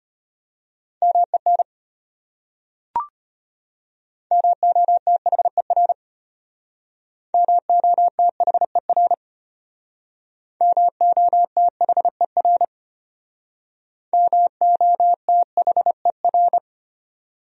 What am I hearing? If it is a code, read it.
Morse code, heard as a keyed tone, about 33 wpm, 700 Hz. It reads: MEN MOTHER MOTHER MOTHER MOTHER